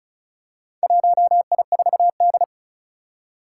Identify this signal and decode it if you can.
Morse code, heard as a keyed tone, about 36 wpm, 700 Hz. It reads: 1I4D
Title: Four Challenging Letter-Number Combinations 35wpm